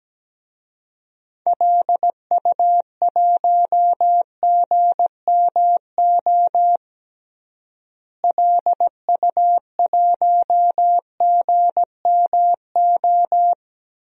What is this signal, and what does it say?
Morse code, heard as a keyed tone, about 17 wpm, 700 Hz. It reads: LU1GMO LU1GMO
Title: DXCC-100 Call Signs - 6 Characters - Repeated 2x - 17wpm